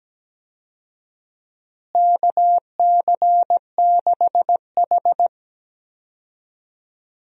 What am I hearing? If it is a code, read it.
Morse code, heard as a keyed tone, about 17 wpm, 700 Hz. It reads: KC6H
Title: USA Call Signs - 4 Characters - Repeated 2x - 17wpm